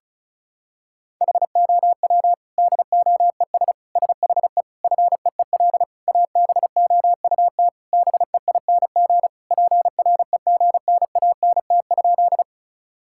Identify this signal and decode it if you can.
Morse code, heard as a keyed tone, about 36 wpm, 700 Hz. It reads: HOW DOES SHE FEEL ABOUT BEING PREGNANT?
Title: Sentences from Top 1000 Words - Review Part 6 - 35wpm